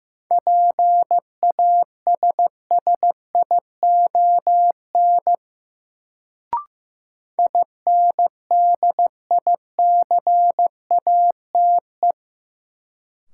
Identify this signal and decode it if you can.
Morse code, heard as a keyed tone, about 15 wpm, 700 Hz. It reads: PASSION INDICATE
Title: Sentences from Top 1300 Words - New Word Drill Alt - 15wpm